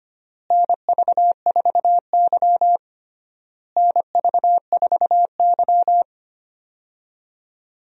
Morse code, heard as a keyed tone, about 25 wpm, 700 Hz. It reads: NV4Y NV4Y
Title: USA Call Signs - 4 Characters - Repeated 3x - 25wpm